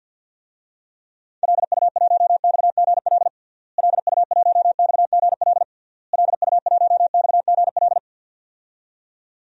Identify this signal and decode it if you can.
Morse code, heard as a keyed tone, about 52 wpm, 700 Hz. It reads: LU1XGL LU1XGL LU1XGL
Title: DXCC-100 Call Signs - 6 Characters - Repeated 3x - 50wpm